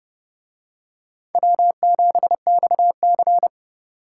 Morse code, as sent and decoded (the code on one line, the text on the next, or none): .-- --... -..- -.-.
W7XC